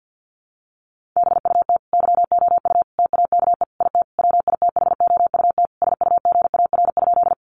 Morse code, heard as a keyed tone, about 49 wpm, 700 Hz. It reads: BUT YOU TAKE IT WITHOUT SUGAR?
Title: Sentences from Top 1200 Words 50wpm